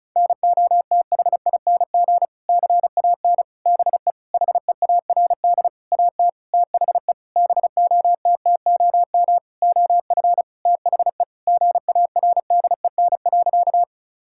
Morse code, heard as a keyed tone, about 36 wpm, 700 Hz. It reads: NOTHING CAN BE HEARD AT THE BOTTOM OF THE GARDEN.
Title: Sentences from Top 1000 Words - Review Part 6 - 35wpm